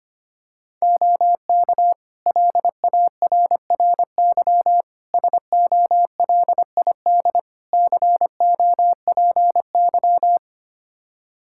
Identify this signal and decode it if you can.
Morse code, heard as a keyed tone, about 25 wpm, 700 Hz. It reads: OK LARRY SOLID COPY
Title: Practice QSO - Exchange 3 - More Info - 25wpm